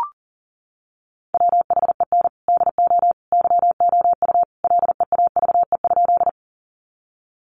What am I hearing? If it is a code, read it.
Morse code, heard as a keyed tone, about 40 wpm, 700 Hz. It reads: WHEN DO YOU LEAVE?
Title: Sentences from Top 400 Words 40wpm